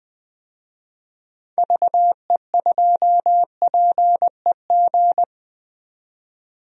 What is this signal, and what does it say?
Morse code, heard as a keyed tone, about 20 wpm, 700 Hz. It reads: VE2PEG
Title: DXCC-100 Call Signs - 6 Characters - 20wpm